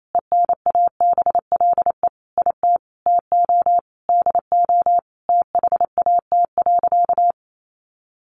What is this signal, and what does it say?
Morse code, heard as a keyed tone, about 28 wpm, 700 Hz. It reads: ENABLE IT TO DO THAT.
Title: Sentences from Top 1500 Words 28wpm